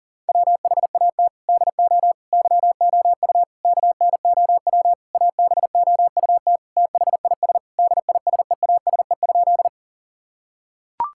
Morse code, heard as a keyed tone, about 40 wpm, 700 Hz. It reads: WHAT DO YOU KNOW ABOUT THIS DISEASE?
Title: Sentences from Top 1000 Words - Review Part 6 - 40wpm